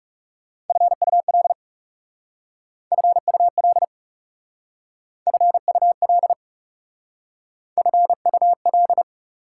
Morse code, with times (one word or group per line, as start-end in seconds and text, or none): 0.7-1.5 FUL
2.9-3.8 FUL
5.3-6.3 FUL
7.8-9.0 FUL